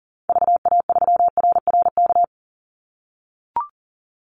Morse code, heard as a keyed tone, about 40 wpm, 700 Hz. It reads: VA3RRK
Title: Call Signs - 40wpm